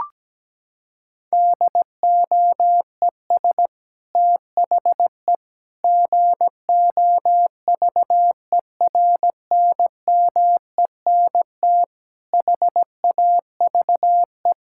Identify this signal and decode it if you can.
Morse code, heard as a keyed tone, about 17 wpm, 700 Hz. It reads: DOES THE GOVERNMENT HAVE
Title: Sentences from Top 1600 Words 17wpm